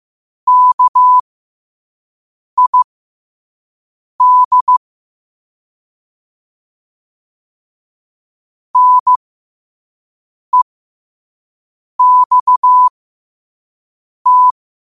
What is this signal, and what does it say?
Morse code, heard as a keyed tone, about 15 wpm, spaced as for 5 wpm, 1000 Hz. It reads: KID NEXT
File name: qotdfiles / 5wpm